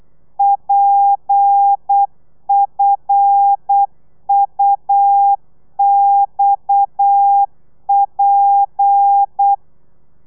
Morse code, heard as a keyed tone, about 8 wpm, 800 Hz. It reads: PFUXP